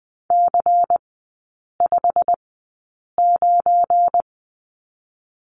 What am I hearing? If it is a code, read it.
Morse code, heard as a keyed tone, about 20 wpm, spaced as for 8 wpm, 700 Hz. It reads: C59
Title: 172 Lesson 28e - Number 9 - Three Characters 20wpm